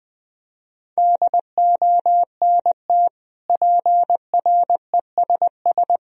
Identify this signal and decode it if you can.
Morse code, heard as a keyed tone, about 21 wpm, 700 Hz. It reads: DONT PRESS
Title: Sentences from Top 1000 Words - Review Part 1 - 20wpm